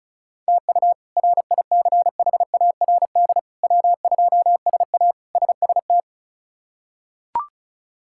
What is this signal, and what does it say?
Morse code, heard as a keyed tone, about 34 wpm, 700 Hz. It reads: TU RICHARD W2SA SST
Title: K1USN SST QSO Practice - 35wpm